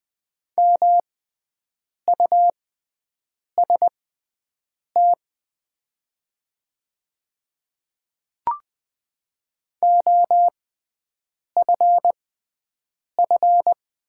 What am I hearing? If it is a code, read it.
Morse code, heard as a keyed tone, about 20 wpm, spaced as for 7 wpm, 700 Hz. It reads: MUST OFF
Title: 6x Character Spacing - Top 200 Words - 20wpm